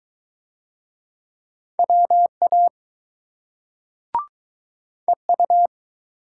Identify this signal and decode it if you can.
Morse code, heard as a keyed tone, about 20 wpm, 700 Hz. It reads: WA EU